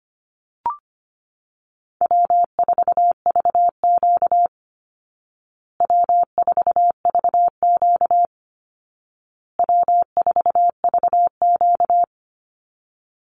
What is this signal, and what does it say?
Morse code, heard as a keyed tone, about 25 wpm, 700 Hz. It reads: W4VQ W4VQ W4VQ
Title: Easy Call Signs Repeated 3x - 25wpm